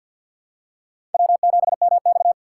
Morse code, heard as a keyed tone, about 51 wpm, 700 Hz. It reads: W7MX
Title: USA Call Signs - 4 Characters - 3x Character Spacing - 50wpm